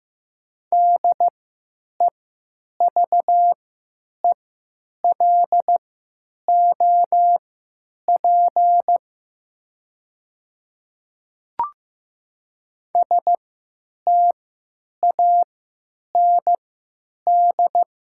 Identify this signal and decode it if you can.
Morse code, heard as a keyed tone, about 15 wpm, spaced as for 9 wpm, 700 Hz. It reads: DEVELOP STAND
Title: 3x Character Spacing - Top 500 Words - 15wpm